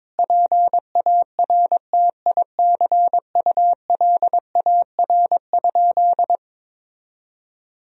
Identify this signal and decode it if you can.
Morse code, heard as a keyed tone, about 22 wpm, 700 Hz. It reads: PARTICULAR?
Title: Sentences from Top 1000 Words - Review Part 4 - 22wpm